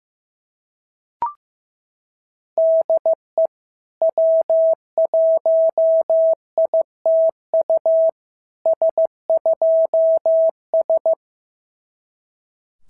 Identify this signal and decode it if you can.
Morse code, heard as a keyed tone, about 16 wpm, 650 Hz. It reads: DE W1ITU S2S
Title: SOTA QSO Practice 15wpm